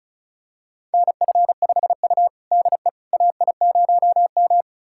Morse code, heard as a keyed tone, about 36 wpm, 700 Hz. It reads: NF5U DE AI0M
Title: USA Callsign DE Callsign - 4 Characters - 35wpm